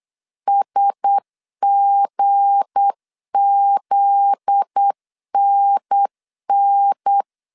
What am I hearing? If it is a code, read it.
Morse code, heard as a keyed tone, about 8 wpm, 800 Hz. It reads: SGZNN
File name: CW / DET / 1 9 1500C S C 8wpm 5653